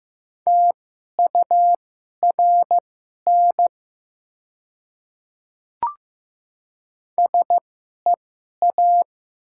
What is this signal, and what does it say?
Morse code, heard as a keyed tone, about 15 wpm, spaced as for 11 wpm, 700 Hz. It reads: TURN SEA
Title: 2x Character Spacing - Top 500 Words - 15wpm